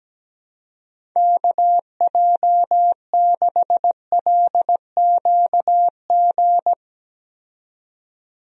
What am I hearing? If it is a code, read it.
Morse code, heard as a keyed tone, about 17 wpm, 700 Hz. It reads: KJ6LQG
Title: USA Call Signs - 6 Characters - 17wpm